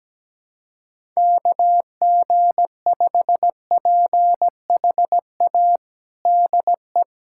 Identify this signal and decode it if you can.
Morse code, heard as a keyed tone, about 17 wpm, 700 Hz. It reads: KG5PHA DE
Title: USA Callsign DE Callsign 17wpm